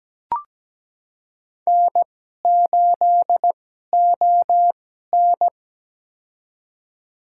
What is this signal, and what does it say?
Morse code, heard as a keyed tone, about 17 wpm, spaced as for 12 wpm, 700 Hz. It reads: N8ON
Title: USA Call Signs - 4 Characters - 2x Character Spacing - 17wpm